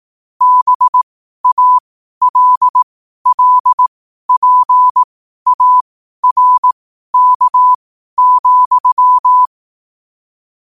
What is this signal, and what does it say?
Morse code, heard as a keyed tone, about 18 wpm, spaced as for 12 wpm, 1000 Hz. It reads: BALLPARK,
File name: qotdfiles / 12wpm